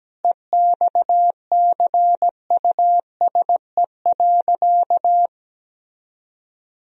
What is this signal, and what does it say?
Morse code, heard as a keyed tone, about 17 wpm, 700 Hz. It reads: EXCUSE.
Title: Sentences from Top 1000 Words - Review Part 4 - 17wpm